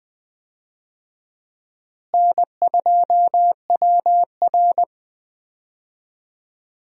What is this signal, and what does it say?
Morse code, heard as a keyed tone, about 20 wpm, 700 Hz. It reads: N2WR
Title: USA Call Signs - 4 Characters - Repeated 3x - 20wpm